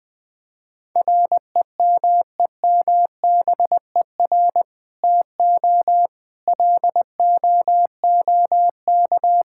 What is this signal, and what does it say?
Morse code, heard as a keyed tone, about 20 wpm, 700 Hz. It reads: REMEMBER TO LOOK
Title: Sentences with 2-Word Phrases - 20wpm